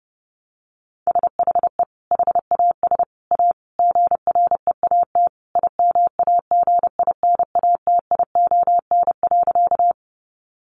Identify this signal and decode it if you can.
Morse code, heard as a keyed tone, about 30 wpm, 700 Hz. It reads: SHE HAS A GREAT IMAGINATION.
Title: Sentences from Top 1300 Words 30wpm